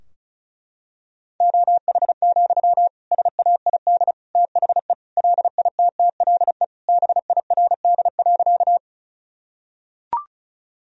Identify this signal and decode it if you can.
Morse code, heard as a keyed tone, about 34 wpm, 700 Hz. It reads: OH, SAID THE LITTLE BIRD.
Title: Sentences from Top 1000 Words - Review Part 2 - 35wpm